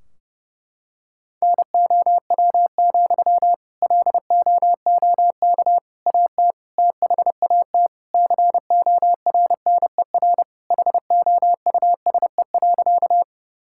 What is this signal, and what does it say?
Morse code, heard as a keyed tone, about 32 wpm, 700 Hz. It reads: NOW, LOOK AT THAT CORNER HOUSE.
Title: Sentences from Top 1600 Words 30wpm